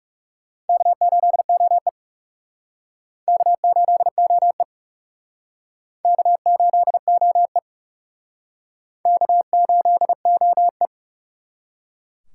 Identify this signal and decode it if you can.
Morse code, heard as a keyed tone, about 36 wpm, 700 Hz. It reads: K8OE K8OE K8OE K8OE
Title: Speed Racing - USA Call Signs 30wpm